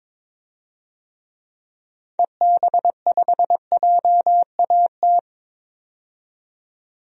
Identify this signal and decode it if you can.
Morse code, heard as a keyed tone, about 23 wpm, 700 Hz. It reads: EB5JAT
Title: DXCC-100 Call Signs - 6 Characters - 22wpm